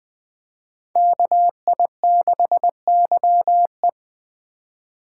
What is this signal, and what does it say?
Morse code, heard as a keyed tone, about 21 wpm, 700 Hz. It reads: KI6YE